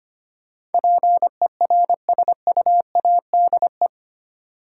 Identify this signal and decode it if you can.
Morse code, heard as a keyed tone, about 25 wpm, 690 Hz. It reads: PERSUADE
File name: 8 Letter Words - First 3 Letters Given - 25wpm